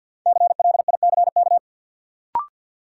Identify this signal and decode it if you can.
Morse code, heard as a keyed tone, about 53 wpm, 700 Hz. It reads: CLICK
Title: Sentences from Top 1100 Words - New Word Drill - 50wpm